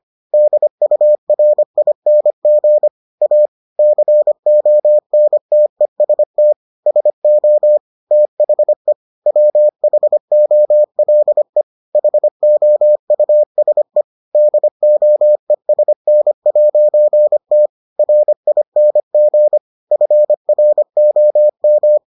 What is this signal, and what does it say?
Morse code, heard as a keyed tone, about 25 wpm, 600 Hz. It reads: DURING A CONTEST SO THE WHOLE HOUSE DOESN'T RING FROM